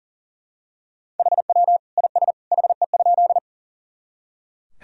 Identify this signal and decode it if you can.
Morse code, heard as a keyed tone, about 40 wpm, 700 Hz. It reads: HW IS HE?
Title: Sentences from Top 100 Words and Abbreviations 40wpm